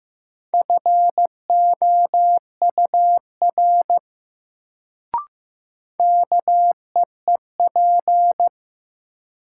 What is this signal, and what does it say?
Morse code, heard as a keyed tone, about 15 wpm, 700 Hz. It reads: FOUR KEEP
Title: Sentences from Top 300 Words - New Word Drill - 15wpm